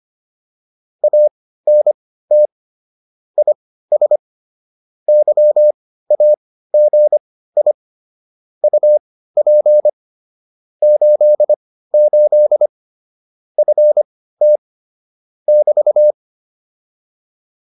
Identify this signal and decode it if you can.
Morse code, heard as a keyed tone, about 25 wpm, spaced as for 15 wpm, 600 Hz. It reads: ANT IS YAGI UP 88 FT <BT>